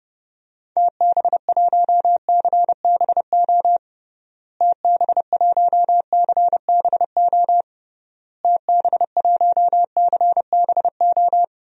Morse code, heard as a keyed tone, about 32 wpm, 700 Hz. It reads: TB1CBO TB1CBO TB1CBO
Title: DXCC-100 Call Signs - 6 Characters - Repeated 3x - 30wpm